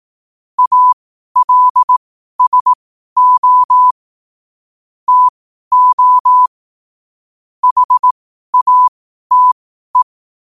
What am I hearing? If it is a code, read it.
Morse code, heard as a keyed tone, about 18 wpm, spaced as for 12 wpm, 1000 Hz. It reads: ALSO TO HATE
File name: qotdfiles / 12wpm